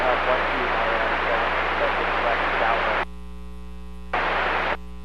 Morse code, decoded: none

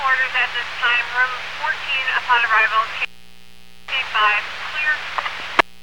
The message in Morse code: none

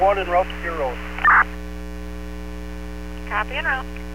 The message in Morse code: none

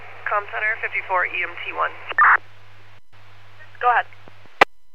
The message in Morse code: none

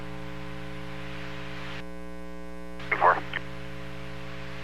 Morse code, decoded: none

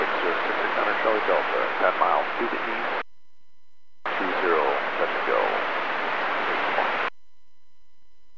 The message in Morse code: none

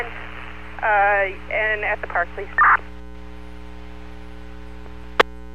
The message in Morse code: none

